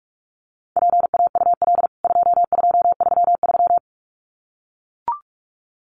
Morse code, heard as a keyed tone, about 44 wpm, 700 Hz. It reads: PAUL 2233
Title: CWT Exchange 45wpm